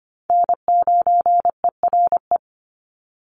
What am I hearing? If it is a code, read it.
Morse code, heard as a keyed tone, about 25 wpm, 700 Hz. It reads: N9ERE
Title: USA Call Signs 25wpm